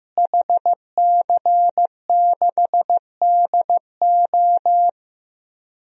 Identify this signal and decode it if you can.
Morse code, heard as a keyed tone, about 15 wpm, 700 Hz. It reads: HC6DO
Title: DXCC-100 Call Signs - 5 Characters - Repeated 3x - 15wpm